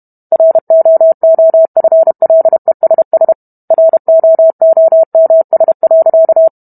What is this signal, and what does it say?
Morse code, heard as a keyed tone, about 32 wpm, 650 Hz. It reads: ROOFLESS ROOMS.